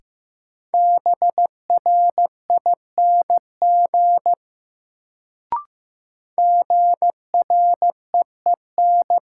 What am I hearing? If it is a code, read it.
Morse code, heard as a keyed tone, about 15 wpm, 700 Hz. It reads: BRING GREEN
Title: Sentences from Top 500 Words - New Word Drill - 15wpm